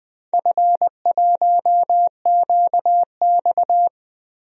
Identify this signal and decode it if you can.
Morse code, heard as a keyed tone, about 20 wpm, 700 Hz. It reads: F1QX
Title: DXCC-100 Call Signs - 4 Characters - 20wpm